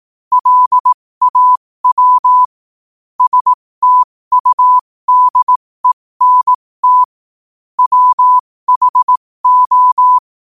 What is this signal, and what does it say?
Morse code, heard as a keyed tone, about 18 wpm, spaced as for 15 wpm, 1000 Hz. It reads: LAW STUDENT WHO
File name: qotdfiles / 15wpm